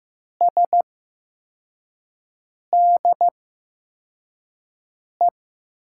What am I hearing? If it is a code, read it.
Morse code, heard as a keyed tone, about 15 wpm, spaced as for 4 wpm, 700 Hz. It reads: SDE
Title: USA Call Signs - 6 Characters - 8x Character Spacing - 15wpm